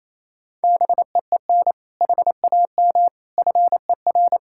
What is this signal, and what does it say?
Morse code, heard as a keyed tone, about 28 wpm, 700 Hz. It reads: BEEN HAM FER
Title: Practice QSO - Exchange 3 - More Info - 28wpm